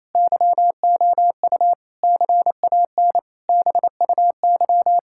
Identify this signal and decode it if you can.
Morse code, heard as a keyed tone, about 28 wpm, 700 Hz. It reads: YOU CAN BUY
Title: Sentences from Top 1000 Words - Review Part 5 - 28wpm